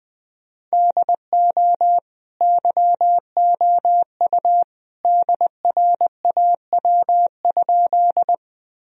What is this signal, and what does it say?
Morse code, heard as a keyed tone, about 20 wpm, 700 Hz. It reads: DO YOU DRAW?
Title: Sentences from Top 1000 Words - Review Part 1 - 20wpm